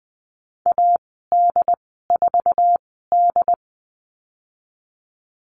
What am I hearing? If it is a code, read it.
Morse code, heard as a keyed tone, about 20 wpm, spaced as for 15 wpm, 700 Hz. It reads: AD4D
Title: 161 Lesson 26g - Letter g - Call Signs 20wpm